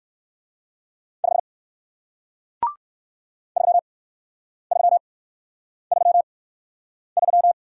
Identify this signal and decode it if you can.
Morse code, heard as a keyed tone, about 56 wpm, spaced as for 9 wpm, 700 Hz. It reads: 5 3333
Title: Speed-Racing Single Letter-Number 45wpm